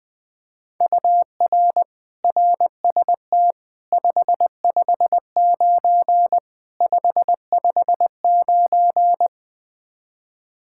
Morse code, heard as a keyed tone, about 20 wpm, 700 Hz. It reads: UR RST 559 559
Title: K1USN SST QSO Practice 20wpm